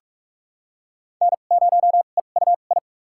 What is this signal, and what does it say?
Morse code, heard as a keyed tone, about 45 wpm, spaced as for 32 wpm, 700 Hz. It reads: N0EUI
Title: USA Call Signs - 5 Characters - 2x Character Spacing - 45wpm